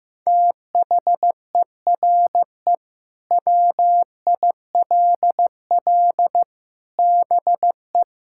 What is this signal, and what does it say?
Morse code, heard as a keyed tone, about 14 wpm, 700 Hz. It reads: THERE WILL BE
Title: Sentences from Top 1600 Words 15wpm